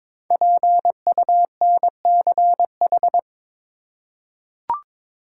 Morse code, heard as a keyed tone, about 22 wpm, 690 Hz. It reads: PUNCH
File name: 5 Letter Words - First 3 Letters Given - 22wpm